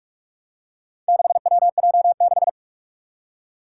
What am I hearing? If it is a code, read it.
Morse code, heard as a keyed tone, about 44 wpm, 700 Hz. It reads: 6WJ6
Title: Four Challenging Letter-Number Combinations 45wpm